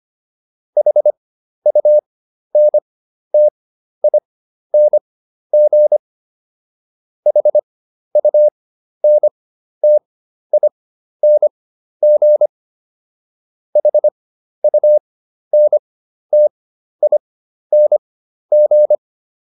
Morse code, heard as a keyed tone, about 25 wpm, spaced as for 12 wpm, 600 Hz. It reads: HUNTING HUNTING HUNTING